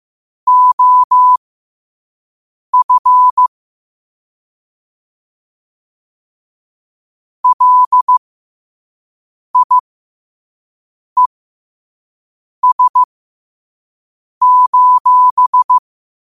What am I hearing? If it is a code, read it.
Morse code, heard as a keyed tone, about 15 wpm, spaced as for 5 wpm, 1000 Hz. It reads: OF LIES:
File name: qotdfiles / 5wpm